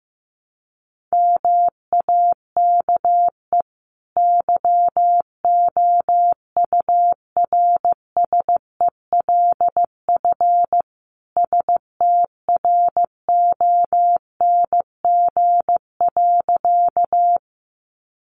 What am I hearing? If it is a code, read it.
Morse code, heard as a keyed tone, about 15 wpm, 700 Hz. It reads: MAKE YOURSELF STRONG.
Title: Sentences from Top 600 Words 15wpm